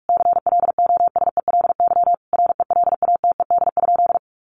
Morse code, heard as a keyed tone, about 44 wpm, 700 Hz. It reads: CLOSELY RELATED?
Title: Sentences from Top 1400 Words 45wpm